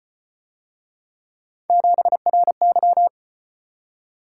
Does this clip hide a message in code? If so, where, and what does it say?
Yes: Morse code from 1.7 to 3.1 s, heard as a keyed tone, about 33 wpm, 700 Hz. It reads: 7RY